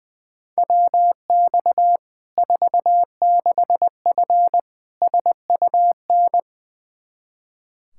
Morse code, heard as a keyed tone, about 20 wpm, 700 Hz. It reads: WX 46F SUN
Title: Practice QSO - Exchange 2 - Info - 20wpm